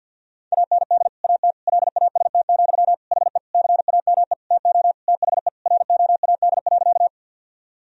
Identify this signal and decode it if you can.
Morse code, heard as a keyed tone, about 51 wpm, 700 Hz. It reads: AND AT LAST, HE CAME TO THE ROAD.
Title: Sentences from Top 1000 Words - Review Part 2 - 50wpm